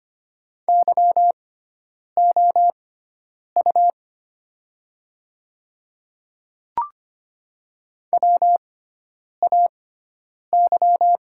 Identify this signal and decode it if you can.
Morse code, heard as a keyed tone, about 25 wpm, spaced as for 9 wpm, 700 Hz. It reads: YOU WAY
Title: 6x Character Spacing - Top 200 Words - 25wpm